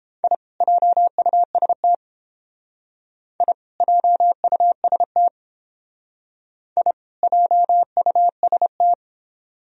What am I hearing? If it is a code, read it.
Morse code, heard as a keyed tone, about 29 wpm, 700 Hz. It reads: I JUST I JUST I JUST